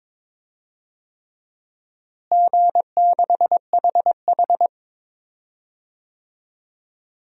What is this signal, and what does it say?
Morse code, heard as a keyed tone, about 22 wpm, 700 Hz. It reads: G6HH
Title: DXCC-100 Call Signs - 4 Characters - Repeated 3x - 22wpm